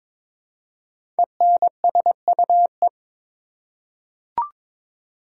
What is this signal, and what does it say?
Morse code, heard as a keyed tone, about 22 wpm, 690 Hz. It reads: ENSUE
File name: 5 Letter Words - First 3 Letters Given - 22wpm